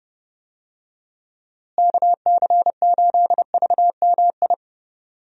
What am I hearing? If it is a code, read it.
Morse code, heard as a keyed tone, about 30 wpm, 700 Hz. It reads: KC8VMI
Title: USA Call Signs - 6 Characters - Repeated 3x - 30wpm